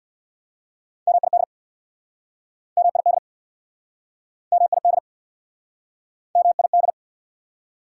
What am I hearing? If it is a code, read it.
Morse code, heard as a keyed tone, about 68 wpm, 690 Hz. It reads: MID MID MID MID